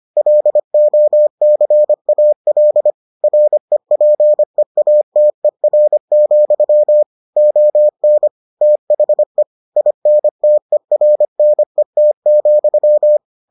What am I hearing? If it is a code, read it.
Morse code, heard as a keyed tone, about 25 wpm, 600 Hz. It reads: LOCAL REPEATER, ON THE INTERNET,